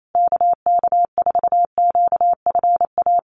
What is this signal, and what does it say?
Morse code, heard as a keyed tone, about 28 wpm, 700 Hz. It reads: KK4QFA
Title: USA Call Signs Repeated 3x 28wpm